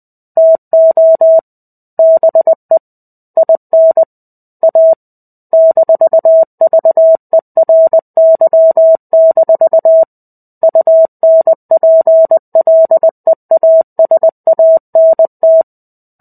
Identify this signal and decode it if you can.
Morse code, heard as a keyed tone, about 20 wpm, 650 Hz. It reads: TO BE IN A -VERY- UNPLEASANT
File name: 05-Chapter-05 20 WPM